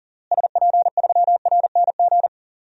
Morse code, heard as a keyed tone, about 40 wpm, 700 Hz. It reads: SP3RNG
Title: DXCC-100 Call Signs - 6 Characters - Repeated 3x - 40wpm